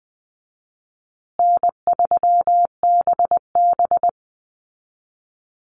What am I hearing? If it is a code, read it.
Morse code, heard as a keyed tone, about 19 wpm, 700 Hz. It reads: N3BB
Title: Easy Call Signs Repeated 2x - 20wpm